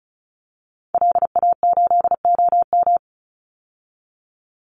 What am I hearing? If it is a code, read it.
Morse code, heard as a keyed tone, about 34 wpm, 700 Hz. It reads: LA8OM
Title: Call Signs - 35wpm